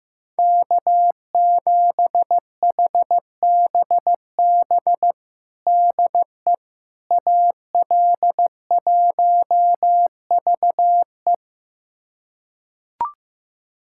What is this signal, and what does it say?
Morse code, heard as a keyed tone, about 15 wpm, 700 Hz. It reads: K7HBB DE AL1VE
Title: USA Callsign DE Callsign - 5 Characters - 15wpm